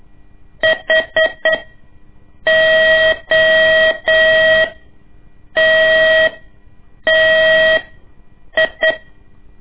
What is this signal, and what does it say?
Morse code, heard as a keyed tone, about 8 wpm, spaced as for 6 wpm, 640 Hz. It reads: HOTTI
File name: ik1weg2 / CORSI CW / lez2-1